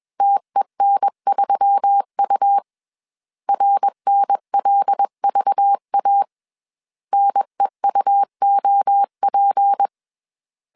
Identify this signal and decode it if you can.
Morse code, heard as a keyed tone, about 21 wpm, 800 Hz. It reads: TEN3U RNLVA NEUOP
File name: CW / DET / 1 6 1500C S B 20wpm 2048